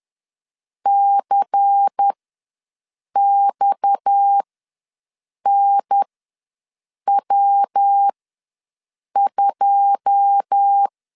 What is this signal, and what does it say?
Morse code, heard as a keyed tone, about 10 wpm, spaced as for 6 wpm, 800 Hz. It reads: CXNW2
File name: LEZIONI / 9 286 ws 10wpm 4032